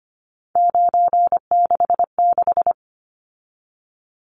Morse code, heard as a keyed tone, about 25 wpm, 700 Hz. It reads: 966
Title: Numbers - 3-Digit 25wpm